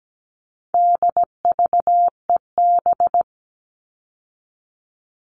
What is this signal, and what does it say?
Morse code, heard as a keyed tone, about 17 wpm, 700 Hz. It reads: DVEB
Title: All Cut Numbers - 4-Digit 17wpm